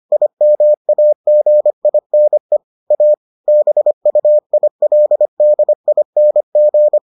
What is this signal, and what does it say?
Morse code, heard as a keyed tone, about 25 wpm, 600 Hz. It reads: IMAGINE A BUILDING